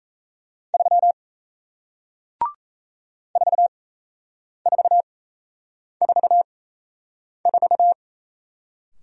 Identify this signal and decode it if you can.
Morse code, heard as a keyed tone, about 38 wpm, spaced as for 8 wpm, 700 Hz. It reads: 3 4444